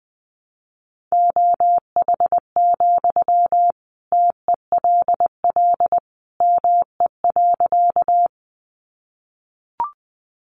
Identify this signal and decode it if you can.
Morse code, heard as a keyed tone, about 20 wpm, 700 Hz. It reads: OH, TELL ME.
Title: Sentences from Top 500 Words 20wpm